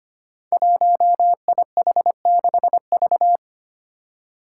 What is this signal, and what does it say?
Morse code, heard as a keyed tone, about 25 wpm, 700 Hz. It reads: 1IH6V
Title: Five Challenging Letter-Number Combinations 25wpm